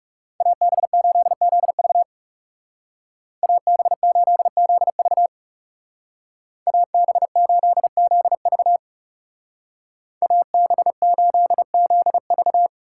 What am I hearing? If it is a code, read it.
Morse code, heard as a keyed tone, about 36 wpm, 700 Hz. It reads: AB8ZV AB8ZV AB8ZV AB8ZV
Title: Speed Racing - USA 2x2 Call Signs 30wpm